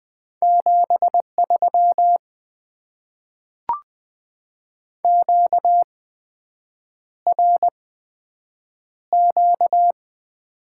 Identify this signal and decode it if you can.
Morse code, heard as a keyed tone, about 20 wpm, spaced as for 7 wpm, 700 Hz. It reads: 73 QRQ
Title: Top 90 QSO Elements - 8x Character Spacing - 20wpm